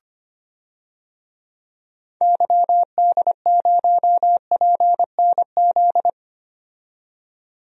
Morse code, heard as a keyed tone, about 25 wpm, 700 Hz. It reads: YD0PNZ
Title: DXCC-100 Call Signs - 6 Characters - 25wpm